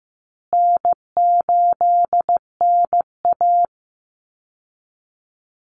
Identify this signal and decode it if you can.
Morse code, heard as a keyed tone, about 14 wpm, 700 Hz. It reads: N8NA
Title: Easy Call Signs Repeated 3x - 15wpm